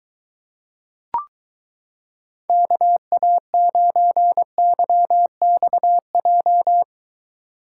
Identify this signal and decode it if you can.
Morse code, heard as a keyed tone, about 22 wpm, 700 Hz. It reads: KA9YXJ